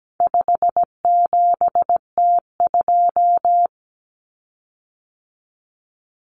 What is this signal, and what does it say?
Morse code, heard as a keyed tone, about 17 wpm, 700 Hz. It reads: 57T2
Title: Common Cut Numbers - 4-Digit 17wpm